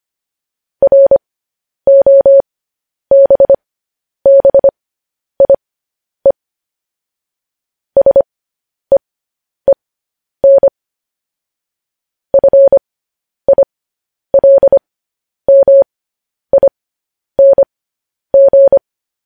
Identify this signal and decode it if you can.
Morse code, heard as a keyed tone, about 25 wpm, spaced as for 10 wpm, 550 Hz. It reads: ROBBIE SEEN FILMING
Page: news headlines in morse code at 10 wpm